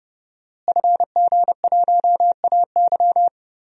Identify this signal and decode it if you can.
Morse code, heard as a keyed tone, about 32 wpm, 700 Hz. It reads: FG1AY
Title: DXCC-100 Call Signs - 5 Characters - Repeated 3x - 30wpm